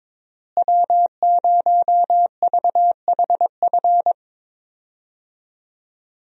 Morse code, heard as a keyed tone, about 21 wpm, 700 Hz. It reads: W0VHF
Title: USA Call Signs - 5 Characters - Repeated 3x - 22wpm